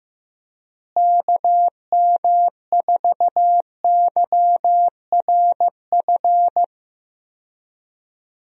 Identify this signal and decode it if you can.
Morse code, heard as a keyed tone, about 15 wpm, 700 Hz. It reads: KM4YRF
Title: USA Call Signs - 6 Characters - Repeated 3x - 15wpm